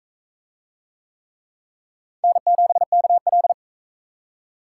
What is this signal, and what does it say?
Morse code, heard as a keyed tone, about 40 wpm, 700 Hz. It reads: N7KL